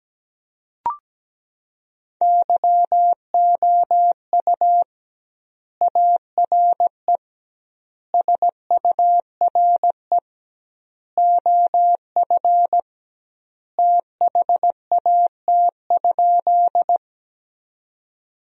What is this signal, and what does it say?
Morse code, heard as a keyed tone, about 17 wpm, 700 Hz. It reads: YOU ARE SURE OF THAT?
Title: 2x Word-Spacing - Sentences from Top 300 Words - 17wpm